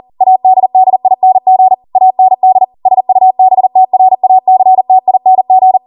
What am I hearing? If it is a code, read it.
Morse code, heard as a keyed tone, about 40 wpm, 750 Hz. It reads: ADDING AND SUBTRACTING
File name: Morse / 40WPM